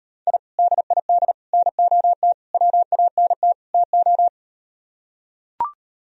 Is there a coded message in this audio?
Yes: Morse code from 0.3 to 4.3 s, heard as a keyed tone, about 39 wpm, 700 Hz. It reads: I DID NOT WANT TO